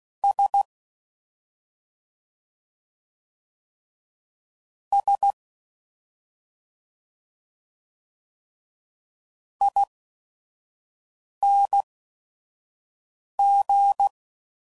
Morse code, heard as a keyed tone, about 16 wpm, spaced as for 5 wpm, 800 Hz. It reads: S S ING
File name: Suffixes / Special ING's / Sngl ING's